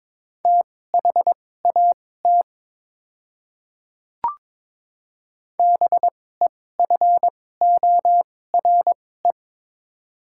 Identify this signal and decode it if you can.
Morse code, heard as a keyed tone, about 22 wpm, spaced as for 16 wpm, 700 Hz. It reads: THAT BEFORE
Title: 2x Character Spacing - Top 200 Words - 22wpm